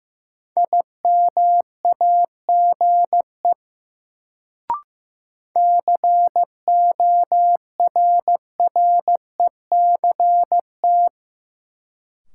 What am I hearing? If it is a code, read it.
Morse code, heard as a keyed tone, about 15 wpm, 700 Hz. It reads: IMAGE CORRECT
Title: Sentences from Top 700 Words - New Word Drill - 15wpm